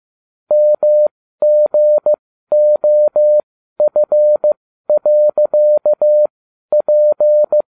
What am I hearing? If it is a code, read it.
Morse code, heard as a keyed tone, about 15 wpm, spaced as for 12 wpm, 600 Hz. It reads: MGOF.P